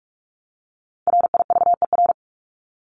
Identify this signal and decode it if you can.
Morse code, heard as a keyed tone, about 44 wpm, 700 Hz. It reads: RIVER
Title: Top 500 Words Repeated 45wpm